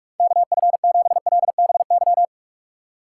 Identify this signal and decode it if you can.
Morse code, heard as a keyed tone, about 46 wpm, 700 Hz. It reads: KF7LBY
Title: USA Call Signs - 6 Characters - 6x Character Spacing - 45wpm